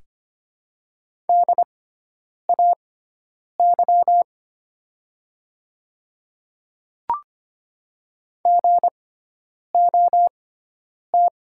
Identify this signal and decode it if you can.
Morse code, heard as a keyed tone, about 25 wpm, spaced as for 9 wpm, 700 Hz. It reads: DAY GOT
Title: 6x Character Spacing - Top 500 Words - 25wpm